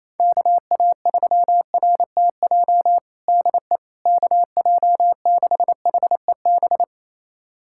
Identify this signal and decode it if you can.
Morse code, heard as a keyed tone, about 28 wpm, 700 Hz. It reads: KA3RTJ DE KJ6HEB
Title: USA Callsign DE Callsign 28wpm